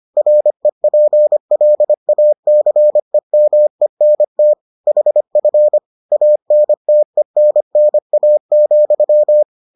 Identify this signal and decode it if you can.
Morse code, heard as a keyed tone, about 25 wpm, 600 Hz. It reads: REPLACEMENT HF ANTENNA,